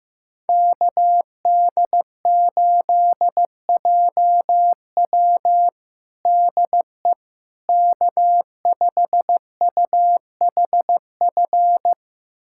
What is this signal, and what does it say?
Morse code, heard as a keyed tone, about 15 wpm, 700 Hz. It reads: KD8JW DE K5UHF
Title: USA Callsign DE Callsign - 5 Characters - 15wpm